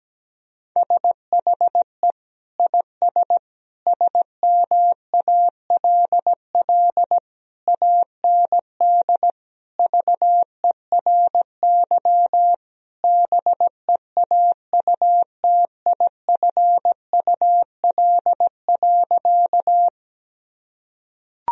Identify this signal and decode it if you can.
Morse code, heard as a keyed tone, about 17 wpm, 700 Hz. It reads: SHE IS SMALL AND VERY BEAUTIFUL.
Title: Sentences from Top 1000 Words - Review Part 1 - 17wpm